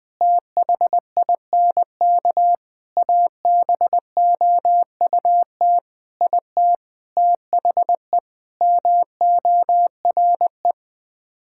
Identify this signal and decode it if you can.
Morse code, heard as a keyed tone, about 20 wpm, 700 Hz. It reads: THINK ABOUT IT THE MORE
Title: Sets of 8 Words - 20wpm